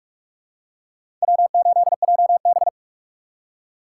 Morse code, heard as a keyed tone, about 46 wpm, 700 Hz. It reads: W8JB
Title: USA Call Signs - 2x Character Spacing - 45wpm